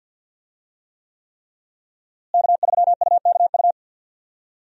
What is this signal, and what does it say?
Morse code, heard as a keyed tone, about 49 wpm, 700 Hz. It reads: K3UKU
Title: USA Call Signs - 5 Characters - Repeated 3x - 50wpm